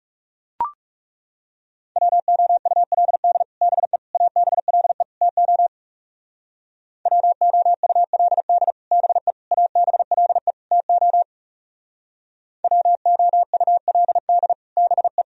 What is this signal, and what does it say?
Morse code, heard as a keyed tone, about 36 wpm, 690 Hz. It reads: WOULD BE ABLE TO WOULD BE ABLE TO WOULD BE
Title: Speed-Racing - Sets of 4 Words 30wpm